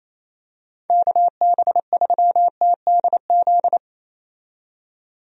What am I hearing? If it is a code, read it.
Morse code, heard as a keyed tone, about 27 wpm, 700 Hz. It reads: KB3TDZ
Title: USA Call Signs - 6 Characters - 28wpm